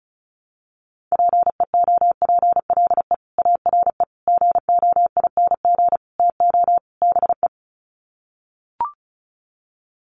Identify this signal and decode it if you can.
Morse code, heard as a keyed tone, about 34 wpm, 700 Hz. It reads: PEOPLE ARE GOING TO BE
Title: Sets of 5 Words Encore 35wpm